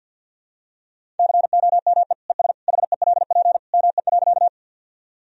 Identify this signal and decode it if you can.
Morse code, heard as a keyed tone, about 50 wpm, 700 Hz. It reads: COME ES HELP ME.
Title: Sentences from Top 200 Words and Abbreviations 50wpm